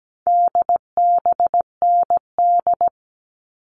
Morse code, heard as a keyed tone, about 17 wpm, 700 Hz. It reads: DBND
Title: All Cut Numbers - 4-Digit 17wpm